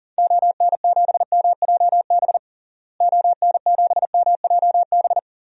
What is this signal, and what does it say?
Morse code, heard as a keyed tone, about 40 wpm, 700 Hz. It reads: ON7MJB ON7MJB
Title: DXCC-100 Call Signs - 6 Characters - Repeated 3x - 40wpm